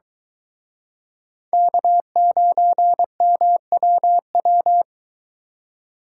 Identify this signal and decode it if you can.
Morse code, heard as a keyed tone, about 23 wpm, 700 Hz. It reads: K9MWW